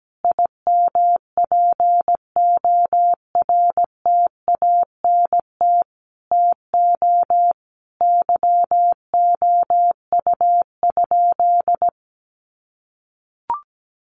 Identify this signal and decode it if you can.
Morse code, heard as a keyed tone, about 17 wpm, 700 Hz. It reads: IMPORTANT TO YOU?
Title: Sentences from Top 1300 Words 17wpm